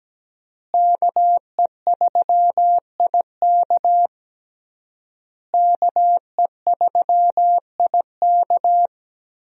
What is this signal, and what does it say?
Morse code, heard as a keyed tone, about 17 wpm, 700 Hz. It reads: KE3IK KE3IK
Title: USA Call Signs - 5 Characters - Repeated 3x - 17wpm